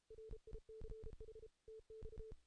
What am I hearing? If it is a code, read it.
Morse code, heard as a keyed tone, about 33 wpm, 440 Hz. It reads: RICH TX
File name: 2016 JAN CW NAQP